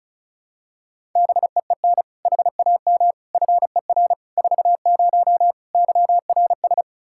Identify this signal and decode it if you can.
Morse code, heard as a keyed tone, about 36 wpm, 700 Hz. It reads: BEEN HAM FER 40 YRS
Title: Practice QSO - Exchange 3 - More Info - 35wpm